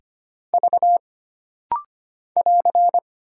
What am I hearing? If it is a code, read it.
Morse code, heard as a keyed tone, about 25 wpm, 700 Hz. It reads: V <AR>
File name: Single Character (Letter:Number:Punctuation:Prosign) - Rapid-Fire - 25wpm